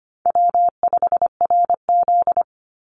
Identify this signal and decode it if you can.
Morse code, heard as a keyed tone, about 25 wpm, 700 Hz. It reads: W5RZ
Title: Easy Call Signs - 25wpm